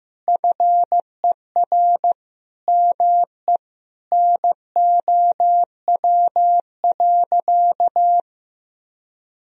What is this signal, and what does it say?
Morse code, heard as a keyed tone, about 16 wpm, 700 Hz. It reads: FER ME NOW.
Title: Sentences from Top 200 Words and Abbreviations 15wpm